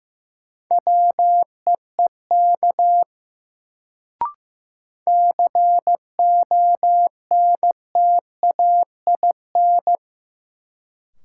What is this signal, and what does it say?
Morse code, heard as a keyed tone, about 15 wpm, 700 Hz. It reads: WEEK CONTAIN
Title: Sentences from Top 500 Words - New Word Drill - 15wpm